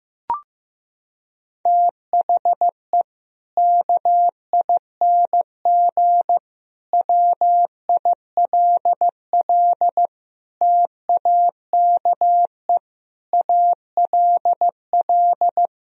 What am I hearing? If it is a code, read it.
Morse code, heard as a keyed tone, about 16 wpm, 700 Hz. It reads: THE KING WILL TAKE ALL
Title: Sentences from Top 1000 Words - Review Part 4 - 15wpm